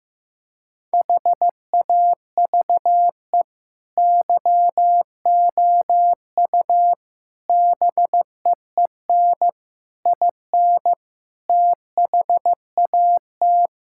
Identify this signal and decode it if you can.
Morse code, heard as a keyed tone, about 15 wpm, 700 Hz. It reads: HAVE YOU BEEN IN THAT
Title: Sentences from Top 1000 Words - Review Part 5 - 15wpm